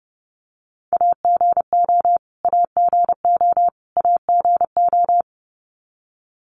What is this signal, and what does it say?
Morse code, heard as a keyed tone, about 30 wpm, 700 Hz. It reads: AGO AGO AGO
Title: Top 500 Words Repeated 30wpm